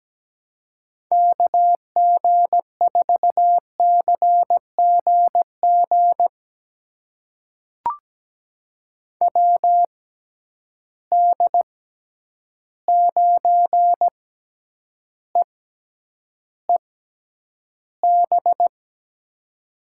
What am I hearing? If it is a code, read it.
Morse code, heard as a keyed tone, about 17 wpm, spaced as for 9 wpm, 700 Hz. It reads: KG4CGG WD9EEB
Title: USA Call Signs - 6 Characters - 6x Character Spacing - 17wpm